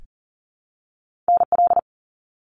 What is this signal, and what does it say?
Morse code, heard as a keyed tone, about 40 wpm, 700 Hz. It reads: NL
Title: Canadian Provinces and Territories 40wpm